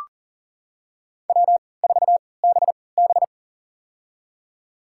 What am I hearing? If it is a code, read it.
Morse code, heard as a keyed tone, about 40 wpm, spaced as for 23 wpm, 700 Hz. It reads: W4BB
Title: USA Call Signs - 4 Characters - 3x Character Spacing - 40wpm